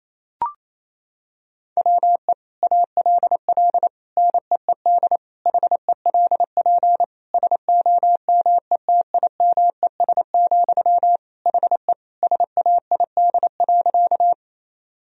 Morse code, heard as a keyed tone, about 28 wpm, 700 Hz. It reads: WE ALL NEED HELP SOMETIMES, HE SAID.
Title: Sentences from Top 1000 Words - Review Part 3 - 28wpm